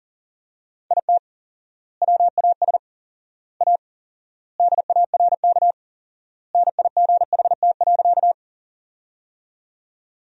Morse code, heard as a keyed tone, about 40 wpm, 700 Hz. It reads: IT WAS A DARK NIGHT.
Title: 4x Word-Spacing - Sentences from Top 500 Words - 40wpm